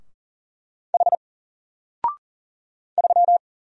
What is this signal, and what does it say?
Morse code, heard as a keyed tone, about 38 wpm, 700 Hz. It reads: H 3
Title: Single Letter-Number Alt 40wpm